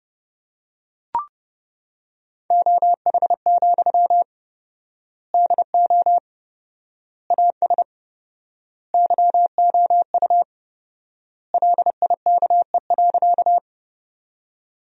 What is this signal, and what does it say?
Morse code, heard as a keyed tone, about 30 wpm, 700 Hz. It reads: OH, DO AS YOU LIKE.
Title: 4x Word-Spacing - Sentences from Top 500 Words - 30wpm